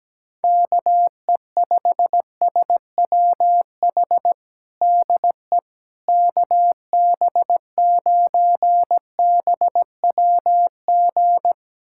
Morse code, heard as a keyed tone, about 17 wpm, 700 Hz. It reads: KE5SWH DE KB9BWG
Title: USA Callsign DE Callsign - 6 Characters - 17wpm